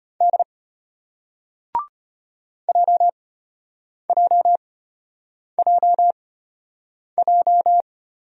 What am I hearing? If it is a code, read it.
Morse code, heard as a keyed tone, about 35 wpm, spaced as for 8 wpm, 700 Hz. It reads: D JJJJ